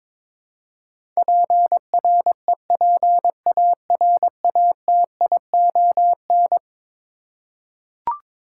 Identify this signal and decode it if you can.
Morse code, heard as a keyed tone, about 22 wpm, 700 Hz. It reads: PREPARATION
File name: practice / 22wpm / Eleven Letter Words - 22wpm